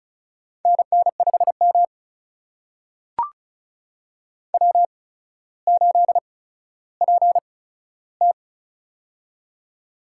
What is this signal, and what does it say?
Morse code, heard as a keyed tone, about 36 wpm, spaced as for 15 wpm, 700 Hz. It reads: NN5M W8PT
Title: USA Call Signs - 4 Characters - 8x Character Spacing - 35wpm